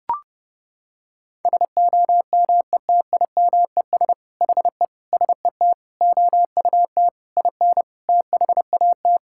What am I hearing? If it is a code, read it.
Morse code, heard as a keyed tone, about 30 wpm, 700 Hz. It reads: SOMETIMES HE SET OUT IN THAT
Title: Sentences from Top 1000 Words - Review Part 5 - 30wpm